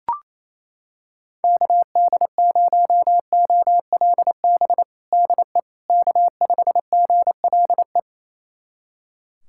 Morse code, heard as a keyed tone, about 28 wpm, 700 Hz. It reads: KD0OLB DE K5GLE
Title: USA Callsign DE Callsign 28wpm